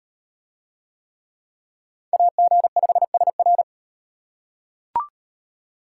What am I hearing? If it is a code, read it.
Morse code, heard as a keyed tone, about 38 wpm, 700 Hz. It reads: AG5SR